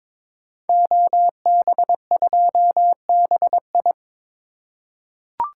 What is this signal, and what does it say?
Morse code, heard as a keyed tone, about 22 wpm, 700 Hz. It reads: OB2BI
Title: Code Groups - 5 Letter:Numbers - 22wpm